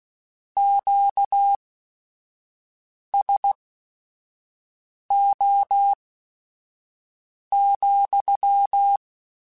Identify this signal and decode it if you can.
Morse code, heard as a keyed tone, about 16 wpm, spaced as for 5 wpm, 800 Hz. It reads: QSO,